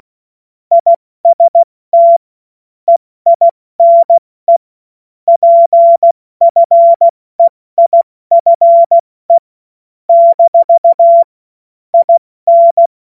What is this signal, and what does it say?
Morse code, heard as a keyed tone, about 16 wpm, 680 Hz. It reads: IST EINE PFEIFE - IN
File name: Morsekurs/Witze